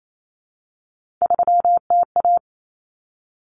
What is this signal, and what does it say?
Morse code, heard as a keyed tone, about 28 wpm, 700 Hz. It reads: 3TA
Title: Common Cut Numbers - 3-Digit 28wpm